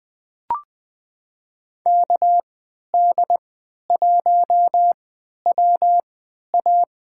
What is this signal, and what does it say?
Morse code, heard as a keyed tone, about 20 wpm, spaced as for 11 wpm, 700 Hz. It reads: KD1WA
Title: USA Call Signs - 3x Character Spacing - 20wpm